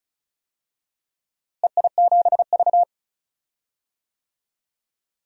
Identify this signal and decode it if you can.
Morse code, heard as a keyed tone, about 36 wpm, 700 Hz. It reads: EI7V
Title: DXCC-100 Call Signs - 4 Characters - Repeated 2x - 35wpm